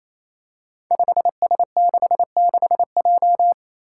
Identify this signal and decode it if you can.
Morse code, heard as a keyed tone, about 29 wpm, 700 Hz. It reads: 5S66J
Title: Five Challenging Letter-Number Combinations 28wpm